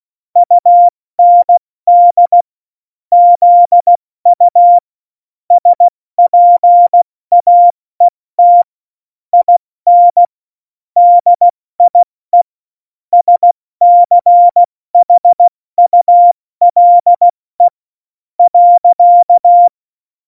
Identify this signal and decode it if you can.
Morse code, heard as a keyed tone, about 16 wpm, spaced as for 14 wpm, 700 Hz. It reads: UND ZU SPAET IN DIE SCHULE.
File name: Morsekurs / Witze / Witz 26. Februar 2019, 70 F80